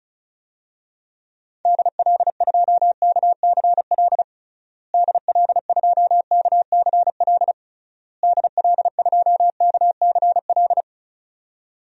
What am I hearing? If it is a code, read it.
Morse code, heard as a keyed tone, about 36 wpm, 700 Hz. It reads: DL2KCL DL2KCL DL2KCL
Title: DXCC-100 Call Signs - 6 Characters - Repeated 3x - 35wpm